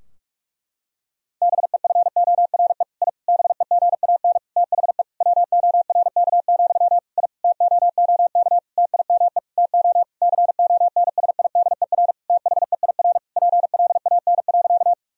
Morse code, heard as a keyed tone, about 46 wpm, 700 Hz. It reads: BEFORE I BEGAN THE WORK, I TOOK TIME TO CONSIDER THEIR PLAN.
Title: Sentences from Top 1000 Words - Review Part 4 - 45wpm